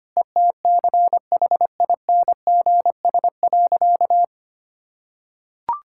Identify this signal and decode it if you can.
Morse code, heard as a keyed tone, about 25 wpm, 700 Hz. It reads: ETCHINGS.
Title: Pangram Challenge v2 - 25wpm